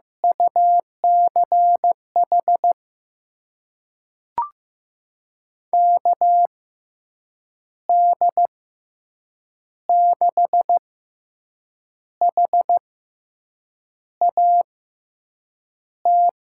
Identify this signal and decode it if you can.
Morse code, heard as a keyed tone, about 15 wpm, spaced as for 6 wpm, 700 Hz. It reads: UCH KD6HAT
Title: USA Call Signs - 6 Characters - 6x Character Spacing - 15wpm